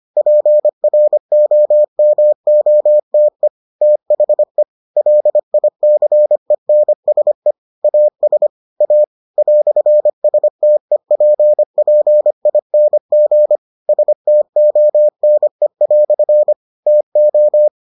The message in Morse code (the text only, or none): PROMOTE THE LICENSE AS A "STEPPING STONE" TO